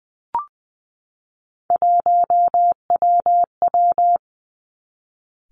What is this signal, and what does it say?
Morse code, heard as a keyed tone, about 20 wpm, 700 Hz. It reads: 1WW
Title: Three Challenging Letter-Number Combinations 20wpm